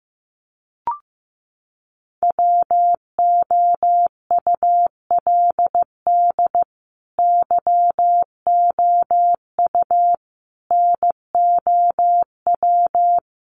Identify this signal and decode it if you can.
Morse code, heard as a keyed tone, about 15 wpm, 700 Hz. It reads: WOULD YOU NOW
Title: Sentences from Top 200 Words 15wpm